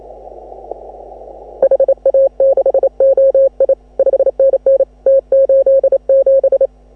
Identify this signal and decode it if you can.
Morse code, heard as a keyed tone, about 28 wpm, 540 Hz. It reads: HA6OI 5NN T87
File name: AUDIO2018 / HA6OI CW